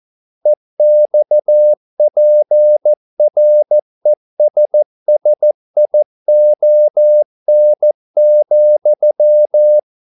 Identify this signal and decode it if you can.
Morse code, heard as a keyed tone, about 14 wpm, 600 Hz. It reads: EXPRESSION,
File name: mp3 files / a dogs mistake - 14wpm